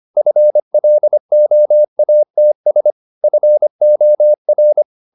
Morse code, heard as a keyed tone, about 25 wpm, 600 Hz. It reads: FLOATS FOR
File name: the old man and the sea part 1 25wpm